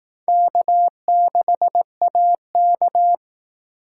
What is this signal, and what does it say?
Morse code, heard as a keyed tone, about 18 wpm, 700 Hz. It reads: K6AK
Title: Speed Racing - USA 1x2 Call Signs 15wpm